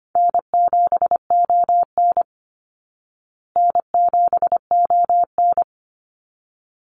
Morse code, heard as a keyed tone, about 25 wpm, 700 Hz. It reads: N7ON N7ON
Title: Call Signs Repeated 3x - 25wpm